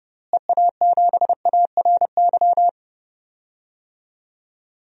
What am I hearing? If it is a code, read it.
Morse code, heard as a keyed tone, about 32 wpm, 700 Hz. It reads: EA7ARY
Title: DXCC-100 Call Signs - 6 Characters - Repeated 3x - 30wpm